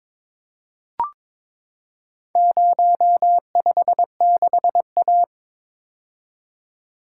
Morse code, heard as a keyed tone, about 22 wpm, 700 Hz. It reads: 056A
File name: practice / 22wpm / Four Letter-Number Combinations - 22wpm